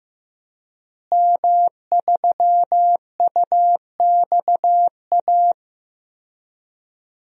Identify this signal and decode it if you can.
Morse code, heard as a keyed tone, about 15 wpm, 700 Hz. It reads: M3UXA
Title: DXCC-100 Call Signs - 5 Characters - Repeated 2x - 15wpm